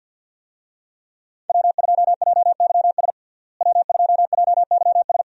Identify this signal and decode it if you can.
Morse code, heard as a keyed tone, about 50 wpm, 700 Hz. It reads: W2JYS W2JYS
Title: USA Call Signs - 5 Characters - Repeated 2x - 50wpm